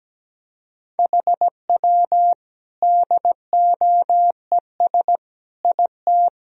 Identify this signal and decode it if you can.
Morse code, heard as a keyed tone, about 17 wpm, 700 Hz. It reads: HW DOES IT
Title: Sentences from Top 400 Words and Abbreviations 17wpm